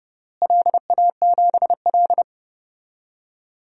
Morse code, heard as a keyed tone, about 32 wpm, 700 Hz. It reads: LA7L
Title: DXCC-100 Call Signs - 4 Characters - Repeated 3x - 30wpm